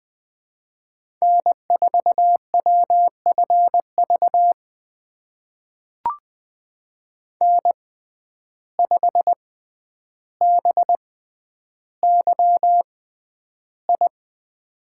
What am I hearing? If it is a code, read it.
Morse code, heard as a keyed tone, about 20 wpm, spaced as for 10 wpm, 700 Hz. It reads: N4WFV N5BYI
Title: USA Call Signs - 5 Characters - 6x Character Spacing - 20wpm